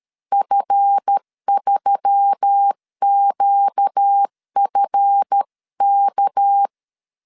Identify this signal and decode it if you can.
Morse code, heard as a keyed tone, about 13 wpm, 800 Hz. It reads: F3QFK
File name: LEZIONI / 7 GF9Z / 7 GF 12wpm 2122